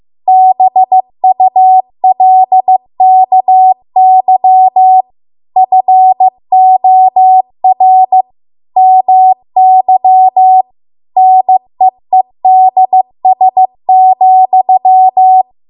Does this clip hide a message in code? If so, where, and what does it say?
Yes: Morse code from 0.3 to 15.4 s, heard as a keyed tone, about 15 wpm, 750 Hz. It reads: BULKY FOR MY NEEDS,